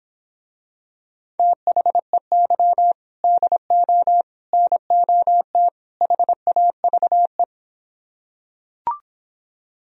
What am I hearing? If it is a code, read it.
Morse code, heard as a keyed tone, about 26 wpm, 700 Hz. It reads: THEY DO NOT HAVE